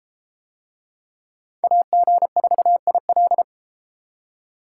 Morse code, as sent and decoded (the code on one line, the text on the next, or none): .- --. ....- .. .-..
AG4IL